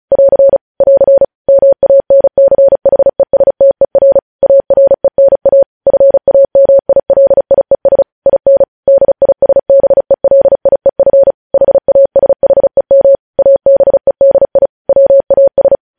Morse code, heard as a keyed tone, about 36 wpm, 550 Hz. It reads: <AR> <AR> MANCHESTER ARENA FAMILIES IN DISBELIEF HASHEM ABEDI WAS
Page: news headlines in morse code at 35 wpm